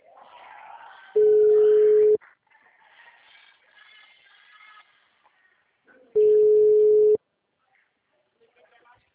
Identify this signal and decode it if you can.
Ringback tone (Europe, 425 Hz), with speech.